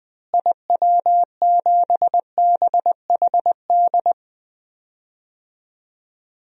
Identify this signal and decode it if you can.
Morse code, heard as a keyed tone, about 20 wpm, 700 Hz. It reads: IW7BHD
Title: DXCC-100 Call Signs - 6 Characters - Repeated 3x - 20wpm